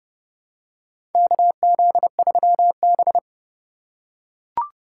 Morse code, heard as a keyed tone, about 30 wpm, 700 Hz. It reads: KZ3B
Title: USA Call Signs - 4 Characters - 2x Character Spacing - 30wpm